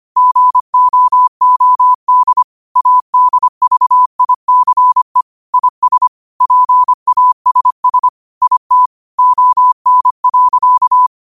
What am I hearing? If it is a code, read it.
Morse code, heard as a keyed tone, about 25 wpm, 1000 Hz. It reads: GOOD ADVICE IS PASS IT ON.
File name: qotdfiles / 25wpm